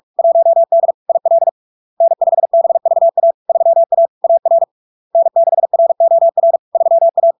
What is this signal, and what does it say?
Morse code, heard as a keyed tone, about 46 wpm, 670 Hz. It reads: 1D IL N5BVA 3A AR N6ROR 3A
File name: LARK-FD-2022-Log-in-CW-45wpm-670-Hz